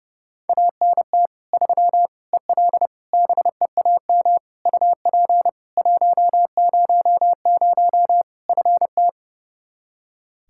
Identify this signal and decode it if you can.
Morse code, heard as a keyed tone, about 30 wpm, 700 Hz. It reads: ANT 3 EL BEAM UP 100 FT
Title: K1USN SST QSO Practice 30wpm